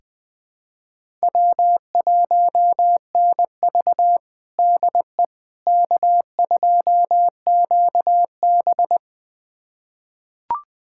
Morse code, heard as a keyed tone, about 20 wpm, 700 Hz. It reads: W1NV DE K2QB
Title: USA Callsign DE Callsign - 4 Characters - 20wpm